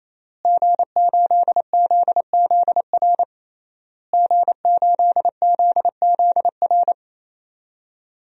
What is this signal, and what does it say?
Morse code, heard as a keyed tone, about 28 wpm, 700 Hz. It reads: G8ZZR G8ZZR
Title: DXCC-100 Call Signs - 5 Characters - Repeated 3x - 28wpm